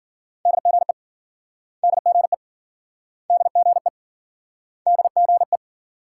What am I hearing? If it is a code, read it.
Morse code, heard as a keyed tone, about 55 wpm, 700 Hz. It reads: DGE DGE DGE DGE